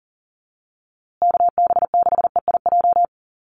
Morse code, heard as a keyed tone, about 40 wpm, 700 Hz. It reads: KB6EIJ
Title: USA Call Signs Repeated 2x 40wpm